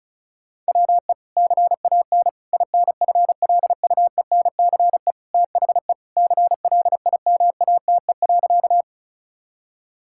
Morse code, heard as a keyed tone, about 34 wpm, 700 Hz. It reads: WE CAN INFLUENCE THE CLIMATE.
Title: Sentences from Top 1600 Words 35wpm